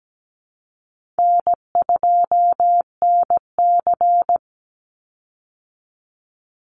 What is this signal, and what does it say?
Morse code, heard as a keyed tone, about 17 wpm, 700 Hz. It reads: N2NC
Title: Easy Call Signs Repeated 2x - 17wpm